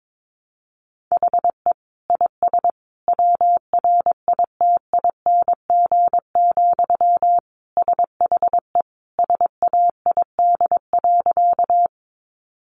Morse code, heard as a keyed tone, about 22 wpm, 700 Hz. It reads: HE IS WRITING, SHE SAID.
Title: Sentences from Top 800 Words 22wpm